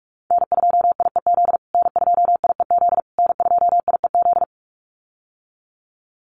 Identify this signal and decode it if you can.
Morse code, heard as a keyed tone, about 44 wpm, 700 Hz. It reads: N2IEZ N2IEZ N2IEZ
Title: USA Call Signs Repeated 3x 45wpm